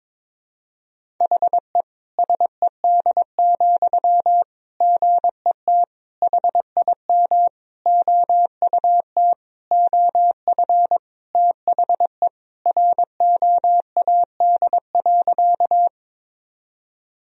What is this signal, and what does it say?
Morse code, heard as a keyed tone, about 22 wpm, 700 Hz. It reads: HE SED, GET HIM OUT OF THE ROAD.
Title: Sentences from Top 400 Words and Abbreviations 22wpm